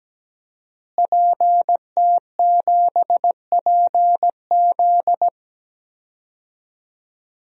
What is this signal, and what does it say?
Morse code, heard as a keyed tone, about 17 wpm, 700 Hz. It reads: PT7PZ
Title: DXCC-100 Call Signs - 5 Characters - Repeated 2x - 17wpm